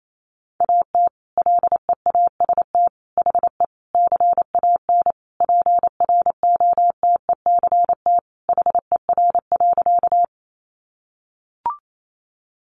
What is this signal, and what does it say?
Morse code, heard as a keyed tone, about 28 wpm, 700 Hz. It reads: AT LEAST HE CAN PROTECT HER.
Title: Sentences from Top 900 Words 28wpm